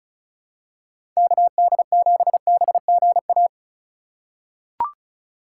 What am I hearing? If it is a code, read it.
Morse code, heard as a keyed tone, about 36 wpm, 700 Hz. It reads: KD7BGA
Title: USA Call Signs - 6 Characters - 4x Character Spacing - 35wpm